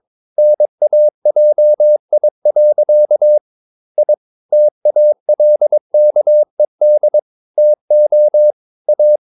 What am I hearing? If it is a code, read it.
Morse code, heard as a keyed tone, about 22 wpm, 600 Hz. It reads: NAJI. I TALKED TO A
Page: Romanen Never af Ken Follet på engelsk.